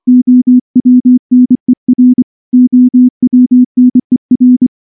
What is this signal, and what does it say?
Morse code, heard as a keyed tone, about 24 wpm, 250 Hz. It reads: OWNER OWNER
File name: morse / morse-training-25WPM